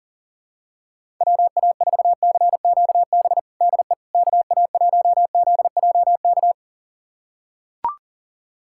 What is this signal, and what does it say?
Morse code, heard as a keyed tone, about 40 wpm, 700 Hz. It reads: WA4CQB DE KA1ZJK
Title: USA Callsign DE Callsign - 6 Characters - 40wpm